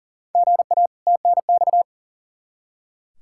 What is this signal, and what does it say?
Morse code, heard as a keyed tone, about 40 wpm, 700 Hz. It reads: GA TNX
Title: Practice QSO - Exchange 1 - Beginning - 40wpm